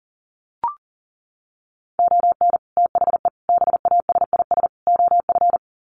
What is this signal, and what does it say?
Morse code, heard as a keyed tone, about 40 wpm, 700 Hz. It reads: ON THE BASIS OF
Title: Sets of 4 Words 40wpm